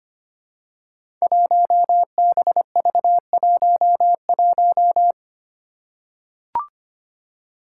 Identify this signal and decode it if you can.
Morse code, heard as a keyed tone, about 25 wpm, 700 Hz. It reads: 1BV11
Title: Code Groups - 5 Letter:Numbers - 25wpm